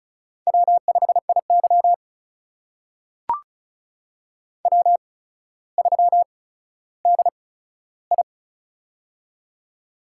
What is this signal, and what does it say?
Morse code, heard as a keyed tone, about 36 wpm, spaced as for 15 wpm, 700 Hz. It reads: W5IY W3DI
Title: USA Call Signs - 4 Characters - 8x Character Spacing - 35wpm